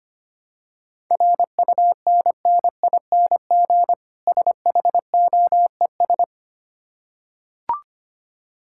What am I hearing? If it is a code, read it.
Morse code, heard as a keyed tone, about 25 wpm, 700 Hz. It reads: RUNNING SHOES
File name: practice / 25wpm / Difficult Words with Hints - 25wpm